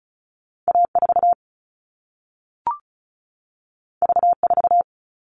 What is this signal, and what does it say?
Morse code, heard as a keyed tone, about 36 wpm, 700 Hz. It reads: A4 V4
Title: All Cut Numbers - 2-Digit 35wpm